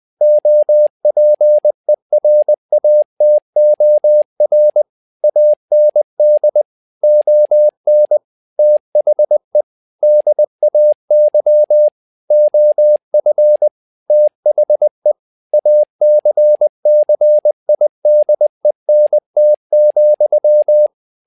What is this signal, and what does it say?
Morse code, heard as a keyed tone, about 20 wpm, 600 Hz. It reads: OPERATOR AND ON THE DAY OF THE ACCIDENT,
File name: tower trouble 20wpm